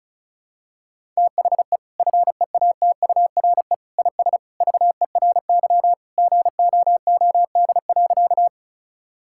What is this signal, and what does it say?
Morse code, heard as a keyed tone, about 36 wpm, 700 Hz. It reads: THE FEATURE IS VERY GOOD.
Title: Sentences from Top 1000 Words - Review Part 5 - 35wpm